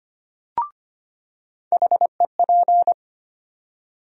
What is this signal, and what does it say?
Morse code, heard as a keyed tone, about 25 wpm, 700 Hz. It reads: HEP
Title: USA Call Sign Postfixes - 3 Letters 25wpm